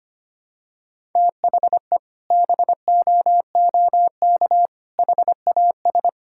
Morse code, heard as a keyed tone, about 25 wpm, 700 Hz. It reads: THE BOOK HAS
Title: Sentences from Top 1000 Words - Review Part 6 - 25wpm